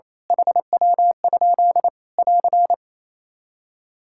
Morse code, heard as a keyed tone, about 28 wpm, 700 Hz. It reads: HW? <AR>
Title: Practice QSO - Exchange 1 - Beginning - 28wpm